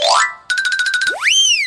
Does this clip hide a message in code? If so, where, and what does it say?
No.